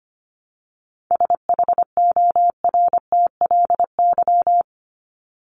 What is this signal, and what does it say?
Morse code, heard as a keyed tone, about 25 wpm, 700 Hz. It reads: SHORTLY
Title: LY Suffix - Top 100 Words with Base Words 25wpm